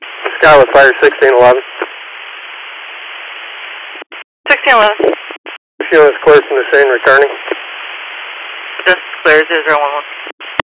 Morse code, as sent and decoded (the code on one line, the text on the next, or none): none